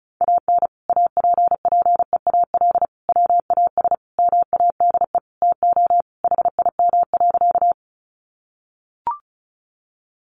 Morse code, heard as a keyed tone, about 34 wpm, 700 Hz. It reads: AN APPEAL WAS MADE TO HIM.
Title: Sentences from Top 1300 Words 35wpm